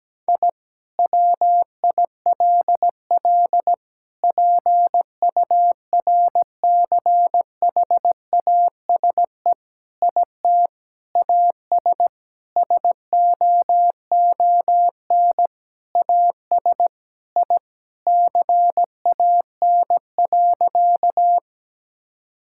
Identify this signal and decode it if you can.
Morse code, heard as a keyed tone, about 17 wpm, 700 Hz. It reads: I WILL PURCHASE IT AS SOON AS I CAN.
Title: Sentences from Top 1000 Words - Review Part 5 - 17wpm